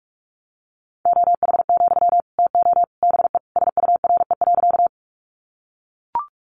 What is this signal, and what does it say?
Morse code, heard as a keyed tone, about 44 wpm, 700 Hz. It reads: OH, TO BE SURE.
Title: Sentences from Top 500 Words 45wpm